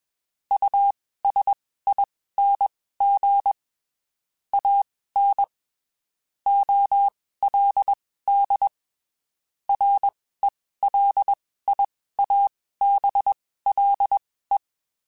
Morse code, heard as a keyed tone, about 21 wpm, spaced as for 15 wpm, 800 Hz. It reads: USING AN OLD RELIABLE
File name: (spaced) / QSOs Spaced-mp3s / WD0FBT 1 @20 sp